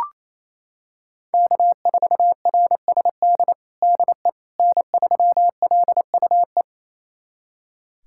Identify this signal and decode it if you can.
Morse code, heard as a keyed tone, about 28 wpm, 700 Hz. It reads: K4RSD DE N3LUE
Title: USA Callsign DE Callsign - 5 Characters - 28wpm